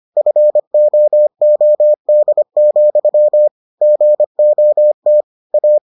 Morse code, heard as a keyed tone, about 25 wpm, 600 Hz. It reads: FOOD, GOT A